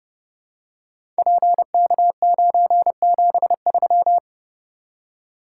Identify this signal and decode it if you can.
Morse code, heard as a keyed tone, about 31 wpm, 700 Hz. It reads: PK973
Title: Code Groups - 5 Letter:Numbers - 30wpm